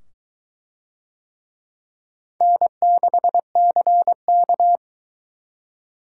Morse code, heard as a keyed tone, about 22 wpm, 700 Hz. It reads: N6CK